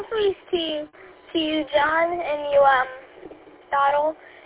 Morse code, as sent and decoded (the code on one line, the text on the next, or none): none